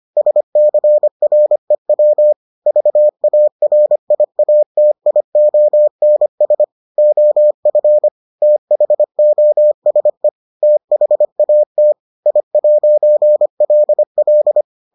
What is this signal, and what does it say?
Morse code, heard as a keyed tone, about 25 wpm, 600 Hz. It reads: SCREW VARIATIONS OF THOSE THAT I'LL